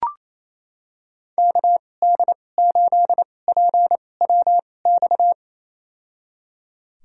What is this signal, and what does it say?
Morse code, heard as a keyed tone, about 28 wpm, spaced as for 20 wpm, 700 Hz. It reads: KD8PWX
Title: USA Call Signs - 6 Characters - 2x Character Spacing - 28wpm